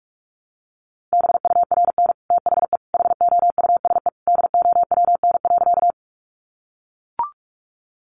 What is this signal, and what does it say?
Morse code, heard as a keyed tone, about 44 wpm, 700 Hz. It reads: BURN THE HOUSE DOWN.
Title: Sentences from Top 1300 Words 45wpm